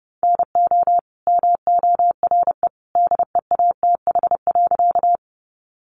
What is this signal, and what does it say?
Morse code, heard as a keyed tone, about 30 wpm, 700 Hz. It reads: NO MORE DEATH.
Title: Sentences from Top 1100 Words 30wpm